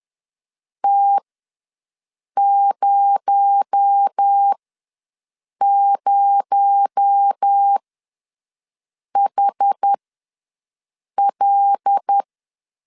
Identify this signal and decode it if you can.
Morse code, heard as a keyed tone, about 10 wpm, spaced as for 5 wpm, 800 Hz. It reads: T00HL
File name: LEZIONI / 6 PKX0 / 6 PKX0 ws 10wpm 4220